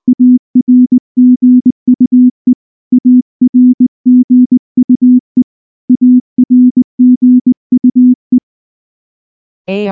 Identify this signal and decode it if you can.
Morse code, heard as a keyed tone, about 20 wpm, 250 Hz. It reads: ARGUE ARGUE ARGUE
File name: morse-training-20WPM